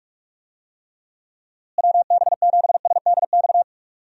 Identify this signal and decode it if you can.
Morse code, heard as a keyed tone, about 47 wpm, 700 Hz. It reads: WB7SDX
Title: USA Call Signs - 6 Characters - Repeated 3x - 45wpm